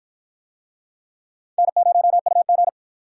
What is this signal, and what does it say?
Morse code, heard as a keyed tone, about 46 wpm, 700 Hz. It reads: N0UG